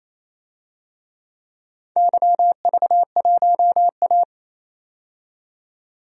Morse code, heard as a keyed tone, about 28 wpm, 700 Hz. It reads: YV1A
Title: DXCC-100 Call Signs - 4 Characters - 28wpm